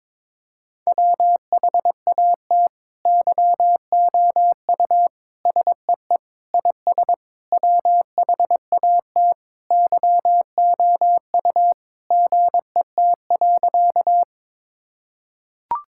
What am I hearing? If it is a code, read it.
Morse code, heard as a keyed tone, about 21 wpm, 700 Hz. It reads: WHAT YOU SEE IS WHAT YOU GET.
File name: Sentences with 7-Word Phrases - 22wpm